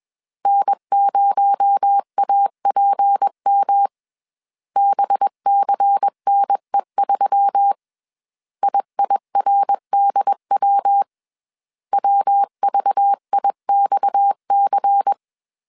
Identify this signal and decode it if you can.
Morse code, heard as a keyed tone, about 21 wpm, 800 Hz. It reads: N0APM BCNE3 IIRDW WVIXC
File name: CW / DET / 1 6 1500C S A 20wpm 2048